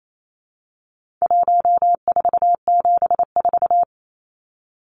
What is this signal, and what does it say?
Morse code, heard as a keyed tone, about 28 wpm, 700 Hz. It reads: 1474
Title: Numbers - 4-Digit 28wpm